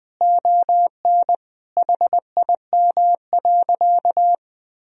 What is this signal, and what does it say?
Morse code, heard as a keyed tone, about 20 wpm, 700 Hz. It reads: ON HIM.
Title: Sentences from Top 1000 Words - Review Part 1 - 20wpm